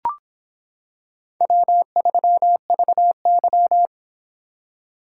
Morse code, heard as a keyed tone, about 26 wpm, 700 Hz. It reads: W3VY